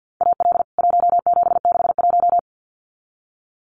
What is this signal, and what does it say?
Morse code, heard as a keyed tone, about 49 wpm, 700 Hz. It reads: AL 1761
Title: CWT Exchange 50wpm